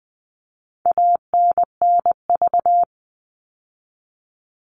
Morse code, heard as a keyed tone, about 20 wpm, 700 Hz. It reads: ANNV
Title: All Cut Numbers - 4-Digit 20wpm